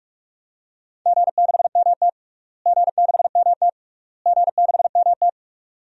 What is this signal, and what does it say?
Morse code, heard as a keyed tone, about 45 wpm, 700 Hz. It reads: G6MT G6MT G6MT
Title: DXCC-100 Call Signs - 4 Characters - Repeated 3x - 45wpm